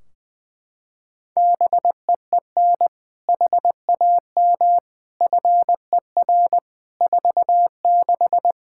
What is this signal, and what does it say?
Morse code, heard as a keyed tone, about 20 wpm, 700 Hz. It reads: BEEN HAM FER 46
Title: K1USN SST QSO Practice 20wpm